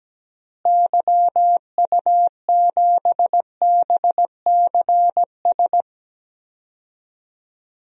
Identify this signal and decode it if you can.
Morse code, heard as a keyed tone, about 17 wpm, 700 Hz. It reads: YU7BCS
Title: DXCC-100 Call Signs - 6 Characters - Repeated 2x - 17wpm